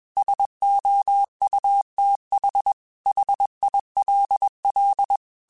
Morse code, heard as a keyed tone, about 21 wpm, 800 Hz. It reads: SOUTH HILL
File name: QSOs / QSOs-mp3s / kc7haa@20